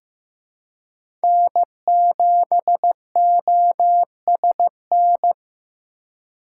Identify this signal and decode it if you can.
Morse code, heard as a keyed tone, about 15 wpm, 700 Hz. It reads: N7OSN
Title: USA Call Signs - 4x Character Spacing - 15wpm